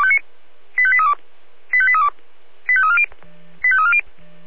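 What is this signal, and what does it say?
Five-tone selective call (ZVEI), heard as quick runs of five tones, tones of about 80 ms. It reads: 85832 85832 85340 85340
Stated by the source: Five-tone ZVEI selective call received by SDR.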